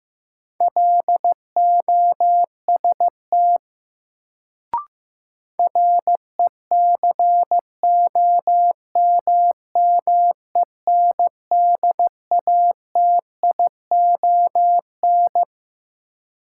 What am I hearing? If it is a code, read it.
Morse code, heard as a keyed tone, about 15 wpm, 700 Hz. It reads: LOST RECOMMENDATION
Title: Sentences from Top 1400 Words - New Word Drill - 15wpm